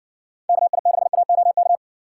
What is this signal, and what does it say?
Morse code, heard as a keyed tone, about 60 wpm, 700 Hz. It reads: KI6AOK